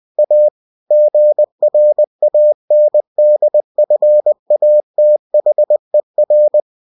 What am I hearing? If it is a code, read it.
Morse code, heard as a keyed tone, about 20 wpm, 600 Hz. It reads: A GRANDFATHER
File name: cw-books / winnie-the-pooh / winnie-20wpm-page-0022